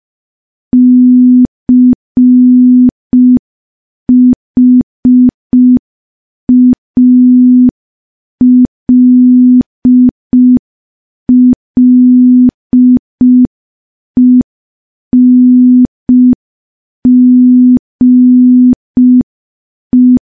Morse code, heard as a keyed tone, about 5 wpm, 250 Hz. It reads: CHALLENGE
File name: morse / morse-training-05WPM